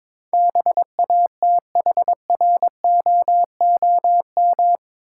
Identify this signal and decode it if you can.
Morse code, heard as a keyed tone, about 22 wpm, 700 Hz. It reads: BATHROOM
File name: practice / 22wpm / Eight Letter Words - 22wpm